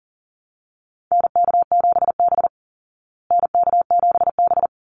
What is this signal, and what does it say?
Morse code, heard as a keyed tone, about 38 wpm, 700 Hz. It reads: NK7B NK7B
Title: Call Signs Repeated 3x - 40wpm